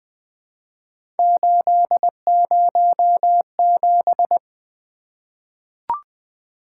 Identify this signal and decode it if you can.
Morse code, heard as a keyed tone, about 20 wpm, 700 Hz. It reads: 807
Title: Three Letter-Number Combinations - 20wpm